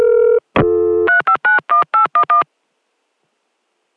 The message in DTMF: 3591811